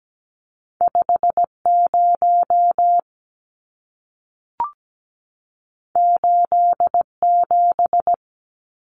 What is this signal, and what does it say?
Morse code, heard as a keyed tone, about 17 wpm, 700 Hz. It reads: 50 87
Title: Numbers - 2-Digit 17wpm